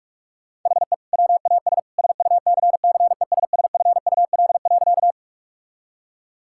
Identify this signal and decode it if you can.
Morse code, heard as a keyed tone, about 42 wpm, 700 Hz. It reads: HE WAS SUCCESSFUL.
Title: Sentences from Top 1000 Words - Review Part 6 - 45wpm